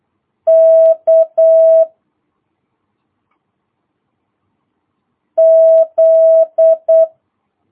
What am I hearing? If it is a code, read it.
Morse code, heard as a keyed tone, about 8 wpm, 650 Hz. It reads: K Z